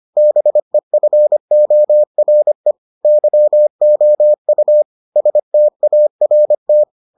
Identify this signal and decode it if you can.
Morse code, heard as a keyed tone, about 25 wpm, 600 Hz. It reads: BEFORE YOU START